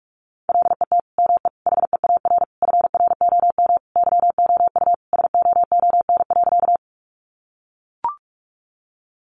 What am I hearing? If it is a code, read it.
Morse code, heard as a keyed tone, about 44 wpm, 700 Hz. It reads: LET ME HEAR FROM YOU SOON.
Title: Sentences from Top 400 Words 45wpm